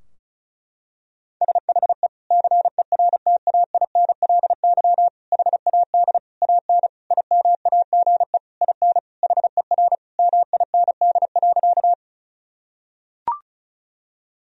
Morse code, heard as a keyed tone, about 36 wpm, 700 Hz. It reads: SHE CERTAINLY HAD AN IMAGE IN HER MIND.
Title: Sentences from Top 1000 Words - Review Part 4 - 35wpm